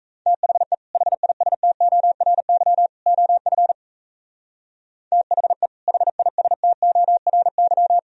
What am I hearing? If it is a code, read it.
Morse code, heard as a keyed tone, about 36 wpm, 700 Hz. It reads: THE HISTORY OF THE HISTORY